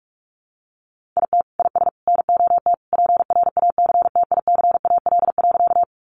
Morse code, heard as a keyed tone, about 44 wpm, 700 Hz. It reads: IT IS NOT PRACTICAL.
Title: Sentences from Top 1400 Words 45wpm